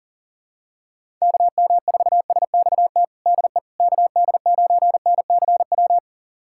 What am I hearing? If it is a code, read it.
Morse code, heard as a keyed tone, about 40 wpm, 700 Hz. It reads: KM4SXT DE KD9NCW
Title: USA Callsign DE Callsign - 6 Characters - 40wpm